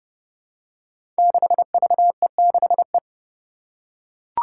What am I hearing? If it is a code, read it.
Morse code, heard as a keyed tone, about 30 wpm, 700 Hz. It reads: 6VE6E
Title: Five Challenging Letter-Number Combinations 30wpm